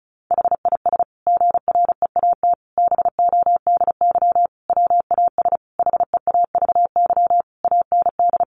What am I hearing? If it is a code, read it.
Morse code, heard as a keyed tone, about 34 wpm, 700 Hz. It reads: HIS GREAT BODY WAS HEAVY AND
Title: Sentences from Top 1400 Words 35wpm